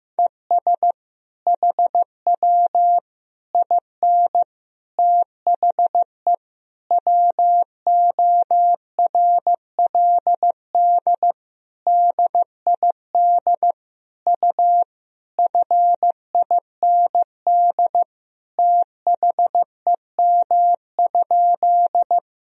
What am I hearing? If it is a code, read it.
Morse code, heard as a keyed tone, about 15 wpm, 700 Hz. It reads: ES HW IN THE WORLD DID U FIND THEM?
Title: Sentences from Top 200 Words and Abbreviations 15wpm